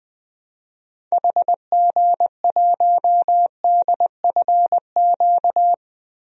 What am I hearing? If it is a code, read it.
Morse code, heard as a keyed tone, about 20 wpm, 700 Hz. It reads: HG1DFQ
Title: DXCC-100 Call Signs - 6 Characters - Repeated 3x - 20wpm